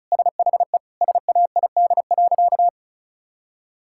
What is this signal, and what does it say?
Morse code, heard as a keyed tone, about 36 wpm, 690 Hz. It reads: SHE SAID.
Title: Sentences from Top 1000 Words - Review Part 6 - 35wpm